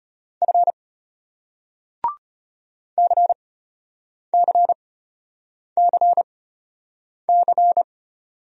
Morse code, heard as a keyed tone, about 34 wpm, spaced as for 8 wpm, 700 Hz. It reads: F CCCC